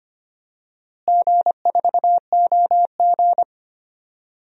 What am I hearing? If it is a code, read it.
Morse code, heard as a keyed tone, about 25 wpm, 700 Hz. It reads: G4OG
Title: DXCC-100 Call Signs - 4 Characters - Repeated 3x - 25wpm